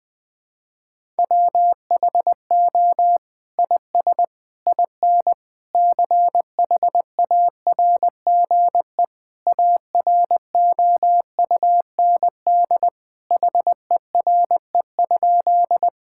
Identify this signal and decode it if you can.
Morse code, heard as a keyed tone, about 21 wpm, 700 Hz. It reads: WHO IS IN CHARGE AROUND HERE?
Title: Sentences from Top 1000 Words - Review Part 6 - 20wpm